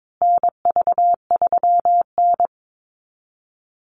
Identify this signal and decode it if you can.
Morse code, heard as a keyed tone, about 22 wpm, 690 Hz. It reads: NV3N
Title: Call Signs Repeated 2x - 22wpm